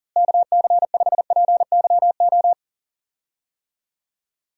Morse code, heard as a keyed tone, about 40 wpm, 700 Hz. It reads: KC5PYO
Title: USA Call Signs - 6 Characters - Repeated 3x - 40wpm